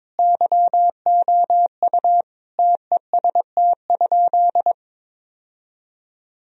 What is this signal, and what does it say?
Morse code, heard as a keyed tone, about 22 wpm, 700 Hz. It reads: YOU TEST?
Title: Sentences from Top 1000 Words - Review Part 3 - 22wpm